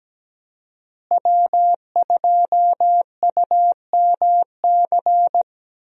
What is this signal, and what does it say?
Morse code, heard as a keyed tone, about 17 wpm, 700 Hz. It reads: W2UMC
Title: USA Call Signs - 5 Characters - Repeated 3x - 17wpm